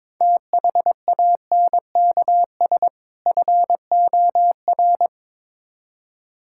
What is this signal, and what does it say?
Morse code, heard as a keyed tone, about 22 wpm, 700 Hz. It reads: THANKS FOR
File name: practice / 22wpm / Build A Phrase - 5-Word Phrases - 22wpm